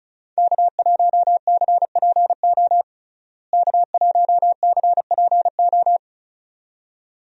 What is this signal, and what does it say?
Morse code, heard as a keyed tone, about 36 wpm, 700 Hz. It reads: K1CPO K1CPO
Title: USA Call Signs - 5 Characters - Repeated 2x - 35wpm